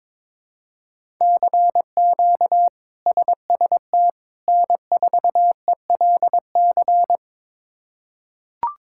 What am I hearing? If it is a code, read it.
Morse code, heard as a keyed tone, about 22 wpm, 700 Hz. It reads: CQ SST N4ELC
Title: K1USN SST QSO Practice - 22wpm